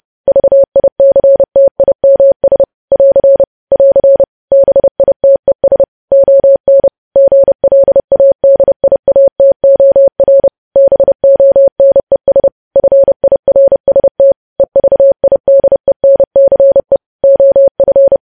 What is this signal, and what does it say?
Morse code, heard as a keyed tone, about 30 wpm, 550 Hz. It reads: VICTIMS <AR> <AR> BITES ON GLADIATOR BONES FIRST EVIDENCE OF
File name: podcast / News headlines-1x-30x30